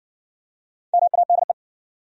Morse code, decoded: MADE

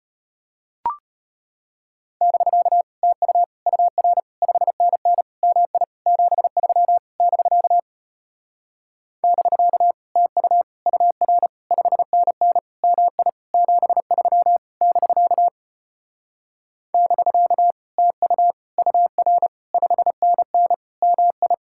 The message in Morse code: <BK> TU UR 5NN MI 73 <BK> <BK> TU UR 5NN MI 73 <BK> <BK> TU UR 5NN MI

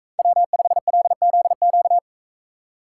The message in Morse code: W5LZQ

